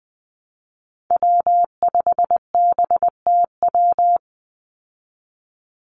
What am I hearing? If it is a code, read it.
Morse code, heard as a keyed tone, about 20 wpm, 700 Hz. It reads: W5BTW
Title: USA Call Signs 20wpm